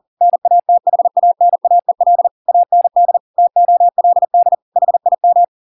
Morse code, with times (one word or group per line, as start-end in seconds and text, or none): 0.2-2.3 NATHANAEL
2.5-3.2 AND
3.4-4.5 TOLD
4.8-5.4 HIM